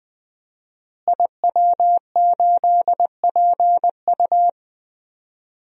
Morse code, heard as a keyed tone, about 20 wpm, 700 Hz. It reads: IW8PU
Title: DXCC-100 Call Signs - 5 Characters - Repeated 2x - 20wpm